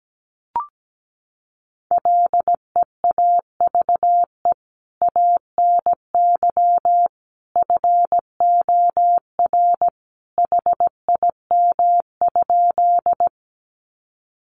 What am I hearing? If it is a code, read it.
Morse code, heard as a keyed tone, about 17 wpm, 700 Hz. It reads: LEAVE ANY FOR HIM?
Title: Sentences from Top 400 Words 17wpm